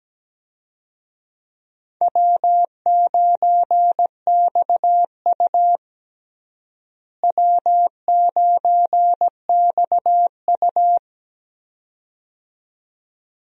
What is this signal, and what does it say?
Morse code, heard as a keyed tone, about 17 wpm, 700 Hz. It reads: W9XU W9XU
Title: USA Call Signs - 4 Characters - Repeated 2x - 17wpm